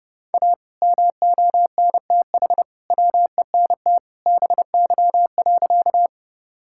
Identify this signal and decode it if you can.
Morse code, heard as a keyed tone, about 30 wpm, 700 Hz. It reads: A MONTH WENT BY.
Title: Sentences from Top 1000 Words - Review Part 5 - 30wpm